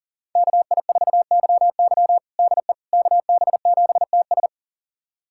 Morse code, heard as a keyed tone, about 40 wpm, 700 Hz. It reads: KI4YY DE KB7TS
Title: USA Callsign DE Callsign - 5 Characters - 40wpm